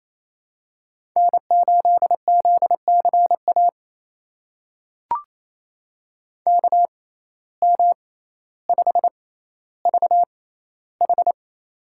Morse code, heard as a keyed tone, about 28 wpm, spaced as for 14 wpm, 700 Hz. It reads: N8ZCA KM5VH
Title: USA Call Signs - 5 Characters - 6x Character Spacing - 28wpm